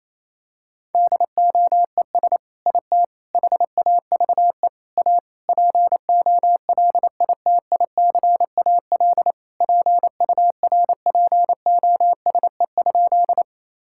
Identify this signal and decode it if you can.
Morse code, heard as a keyed tone, about 28 wpm, 700 Hz. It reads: DOES IT HAVE A POLITICAL PURPOSE?
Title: Sentences from Top 1000 Words - Review Part 5 - 28wpm